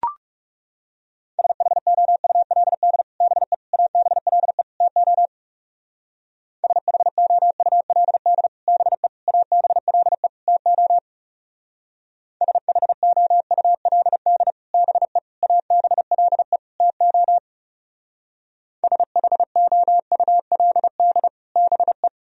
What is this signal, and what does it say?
Morse code, heard as a keyed tone, about 38 wpm, 700 Hz. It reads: SHOULD BE ABLE TO SHOULD BE ABLE TO SHOULD BE ABLE TO SHOULD BE